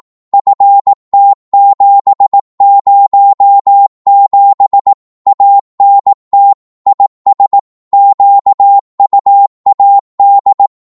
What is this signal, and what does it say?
Morse code, heard as a keyed tone, about 18 wpm, 800 Hz. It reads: FT707 ANT IS QUAD
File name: Morse / qso-18wpm